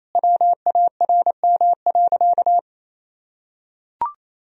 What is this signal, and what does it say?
Morse code, heard as a keyed tone, about 28 wpm, 700 Hz. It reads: WARM.
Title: Sentences from Top 1000 Words - Review Part 2 - 28wpm